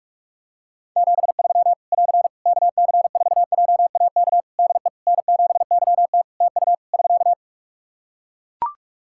Morse code, heard as a keyed tone, about 41 wpm, 700 Hz. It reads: 73 <AR> KC4JAK DE N7YT TU <SK>
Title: Practice QSO - Exchange 4 - Ending - 45wpm